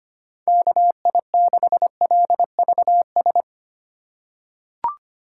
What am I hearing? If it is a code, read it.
Morse code, heard as a keyed tone, about 25 wpm, 700 Hz. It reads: KI6LVS
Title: USA Call Signs - 8x Character Spacing - 25wpm